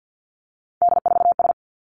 Morse code, heard as a keyed tone, about 50 wpm, 700 Hz. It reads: D4S
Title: Three Challenging Letter-Number Combinations 50wpm